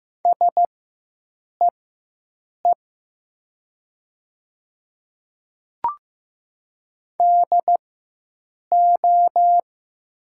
Morse code, heard as a keyed tone, about 15 wpm, spaced as for 7 wpm, 700 Hz. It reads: SEE DO
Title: 4x Character Spacing - Top 300 Words - 15wpm